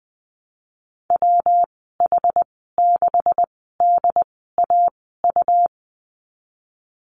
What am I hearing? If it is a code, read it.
Morse code, heard as a keyed tone, about 20 wpm, spaced as for 14 wpm, 700 Hz. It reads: WH6DAU
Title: 128 Lesson 21g - Number 6 - Call Signs 20wpm